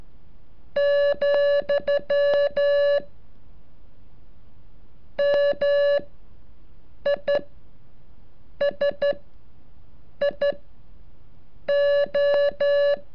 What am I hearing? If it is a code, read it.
Morse code, heard as a keyed tone, about 14 wpm, spaced as for 6 wpm, 570 Hz. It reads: , MISIO
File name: curso / ejercicio 36